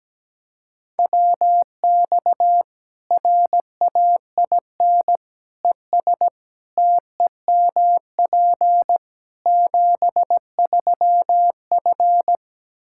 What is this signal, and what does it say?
Morse code, heard as a keyed tone, about 17 wpm, 700 Hz. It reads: WX RAIN ES TEMP 73F
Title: Practice QSO - Exchange 2 - Info - 17wpm